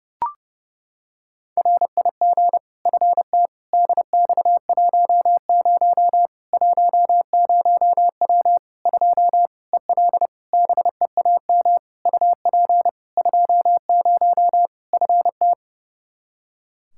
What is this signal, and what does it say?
Morse code, heard as a keyed tone, about 30 wpm, 700 Hz. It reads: RIG FT DX10 10W 2 EL BEAM UP 20 FT
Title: Practice QSO - Exchange 2 - Info - 30wpm